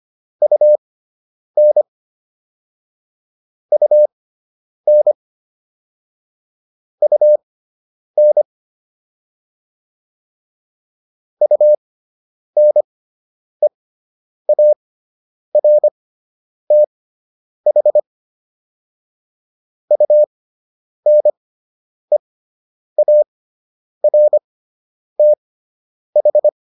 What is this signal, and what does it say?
Morse code, heard as a keyed tone, about 25 wpm, spaced as for 9 wpm, 600 Hz. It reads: UN UN UN UNEARTH UNEARTH